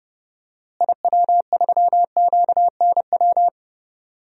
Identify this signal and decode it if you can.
Morse code, heard as a keyed tone, about 30 wpm, 700 Hz. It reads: IW3QNW
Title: DXCC-100 Call Signs - 6 Characters - Repeated 3x - 30wpm